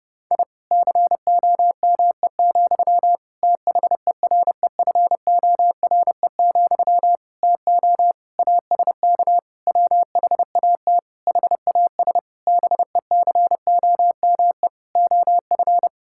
Morse code, heard as a keyed tone, about 30 wpm, 700 Hz. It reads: I COME, THEREFORE, TO ASK WHAT HAS BECOME OF
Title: Sentences from Top 1000 Words - Review Part 5 - 30wpm